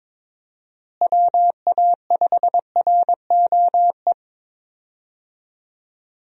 Morse code, heard as a keyed tone, about 22 wpm, 700 Hz. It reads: WA5ROE
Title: USA Call Signs - 6 Characters - Repeated 3x - 22wpm